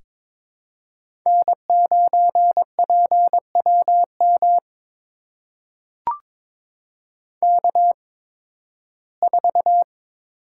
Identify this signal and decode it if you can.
Morse code, heard as a keyed tone, about 22 wpm, spaced as for 14 wpm, 700 Hz. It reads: N9PWM K4
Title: USA Call Signs - 5 Characters - 8x Character Spacing - 22wpm